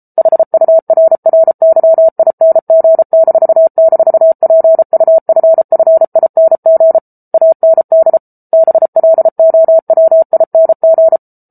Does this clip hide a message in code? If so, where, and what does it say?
Yes: Morse code from 0.2 to 11.2 s, heard as a keyed tone, about 34 wpm, 650 Hz. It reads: HURRYING--PUFFING AND BLOWING